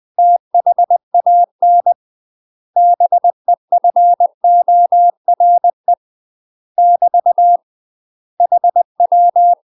Morse code, heard as a keyed tone, about 20 wpm, 700 Hz. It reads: THAN BEFORE <BT> HW
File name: UK MASTER QSOs TRAINING / 20WPM / UK MASTER QSOs 0073